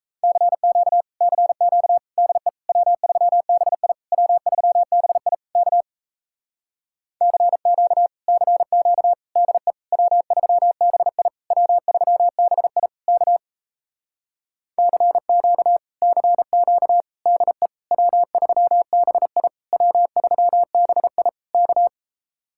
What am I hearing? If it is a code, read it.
Morse code, heard as a keyed tone, about 37 wpm, 700 Hz. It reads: CQ CQ DE W3BI W3BI K CQ CQ DE W3BI W3BI K CQ CQ DE W3BI W3BI K